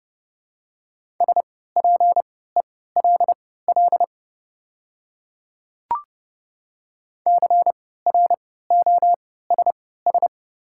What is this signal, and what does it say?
Morse code, heard as a keyed tone, about 29 wpm, spaced as for 17 wpm, 700 Hz. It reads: SPELL CROSS
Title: 3x Character Spacing - Top 300 Words - 30wpm